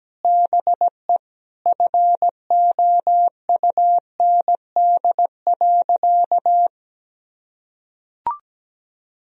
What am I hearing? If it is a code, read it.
Morse code, heard as a keyed tone, about 17 wpm, 700 Hz. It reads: BE FOUND.
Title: Sentences from Top 1000 Words - Review Part 4 - 17wpm